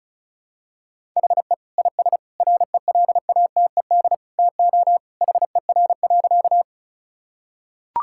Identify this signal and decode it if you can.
Morse code, heard as a keyed tone, about 36 wpm, 700 Hz. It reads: HE IS RELATED TO HER.
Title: Sentences from Top 1000 Words - Review Part 5 - 35wpm